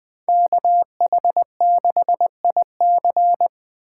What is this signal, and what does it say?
Morse code, heard as a keyed tone, about 21 wpm, 700 Hz. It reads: KH6IC